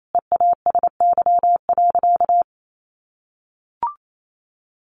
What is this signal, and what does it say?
Morse code, heard as a keyed tone, about 28 wpm, 700 Hz. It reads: EASY.
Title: Sentences from Top 700 Words 28wpm